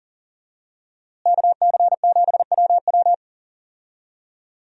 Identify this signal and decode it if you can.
Morse code, heard as a keyed tone, about 39 wpm, 700 Hz. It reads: KC7WW
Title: USA Call Signs - 5 Characters - Repeated 3x - 40wpm